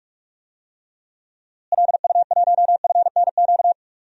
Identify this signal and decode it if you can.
Morse code, heard as a keyed tone, about 43 wpm, 700 Hz. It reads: LU1FNQ
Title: DXCC-100 Call Signs - 6 Characters - Repeated 2x - 45wpm